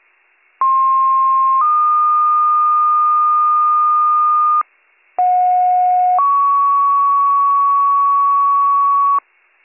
Two-tone page, heard as a one-second tone, then a three-second tone, about 1060 then 1220 Hz, about 730 then 1120 Hz. It is heard from 0.6 to 9.2 s.